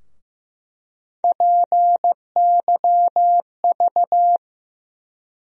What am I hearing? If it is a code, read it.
Morse code, heard as a keyed tone, about 15 wpm, 700 Hz. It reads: PYV
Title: USA Call Sign Postfixes - 3 Letters 15wpm